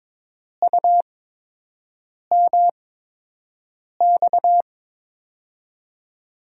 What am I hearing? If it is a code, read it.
Morse code, heard as a keyed tone, about 22 wpm, spaced as for 6 wpm, 700 Hz. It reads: UMX
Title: USA Call Signs - 5 Characters - 8x Character Spacing - 22wpm